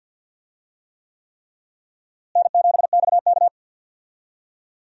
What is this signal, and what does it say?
Morse code, heard as a keyed tone, about 51 wpm, 700 Hz. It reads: N7XK
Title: USA Call Signs - 4 Characters - Repeated 3x - 50wpm